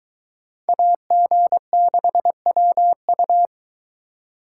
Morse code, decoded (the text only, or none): AG6WU